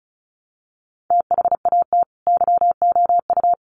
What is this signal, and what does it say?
Morse code, heard as a keyed tone, about 34 wpm, 700 Hz. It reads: THAT YOU
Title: Sets of 2 Words 35wpm